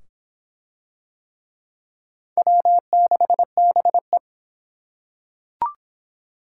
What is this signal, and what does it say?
Morse code, heard as a keyed tone, about 23 wpm, 700 Hz. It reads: W6BE